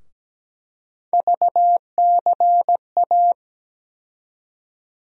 Morse code, heard as a keyed tone, about 17 wpm, 700 Hz. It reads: VCA
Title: USA Call Sign Postfixes - 3 Letters 17wpm